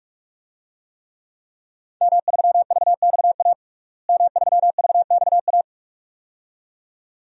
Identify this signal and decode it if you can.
Morse code, heard as a keyed tone, about 46 wpm, 700 Hz. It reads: M3VXA M3VXA
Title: DXCC-100 Call Signs - 5 Characters - Repeated 2x - 45wpm